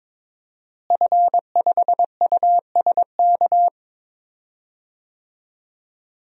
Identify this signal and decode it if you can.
Morse code, heard as a keyed tone, about 22 wpm, 700 Hz. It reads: F5USK
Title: DXCC-100 Call Signs - 5 Characters - Repeated 2x - 22wpm